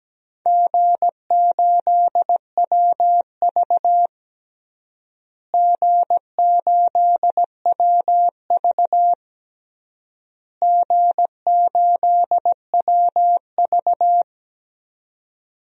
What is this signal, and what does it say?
Morse code, heard as a keyed tone, about 17 wpm, 700 Hz. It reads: G8WV G8WV G8WV
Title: DXCC-100 Call Signs - 4 Characters - Repeated 3x - 17wpm